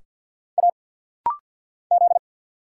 Morse code, heard as a keyed tone, about 50 wpm, 690 Hz. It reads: A Z
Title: Single Letters - Rapid-Fire 50wpm